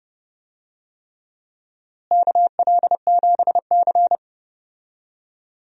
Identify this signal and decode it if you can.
Morse code, heard as a keyed tone, about 29 wpm, 700 Hz. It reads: KL7C
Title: DXCC-100 Call Signs - 4 Characters - Repeated 3x - 30wpm